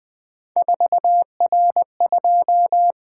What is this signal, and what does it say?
Morse code, heard as a keyed tone, about 20 wpm, 700 Hz. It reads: 4R2
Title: Three Letter-Number Combinations - 20wpm